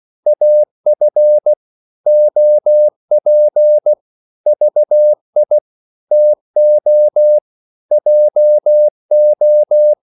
Morse code, heard as a keyed tone, about 16 wpm, 600 Hz. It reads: AF OP VI TO JO
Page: Korte danske ord 16wpm